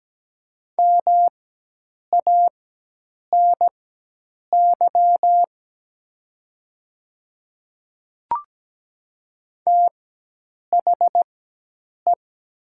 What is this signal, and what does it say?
Morse code, heard as a keyed tone, about 17 wpm, spaced as for 8 wpm, 700 Hz. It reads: MANY THE
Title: 4x Character Spacing - Top 200 Words - 17wpm